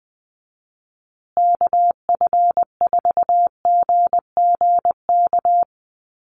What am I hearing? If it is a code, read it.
Morse code, heard as a keyed tone, about 20 wpm, 700 Hz. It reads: KF4GGK
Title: USA Call Signs Repeated 2x 20wpm